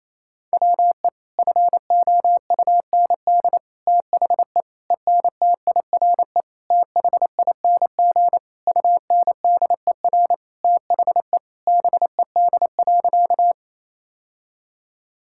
Morse code, heard as a keyed tone, about 28 wpm, 700 Hz. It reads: WE FOUND THE ENTIRE THING UNDER THE BED.
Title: Sentences from Top 1000 Words - Review Part 5 - 28wpm